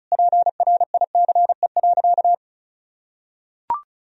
Morse code, heard as a keyed tone, about 34 wpm, 700 Hz. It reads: PRICE.
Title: Sentences from Top 1000 Words - Review Part 4 - 35wpm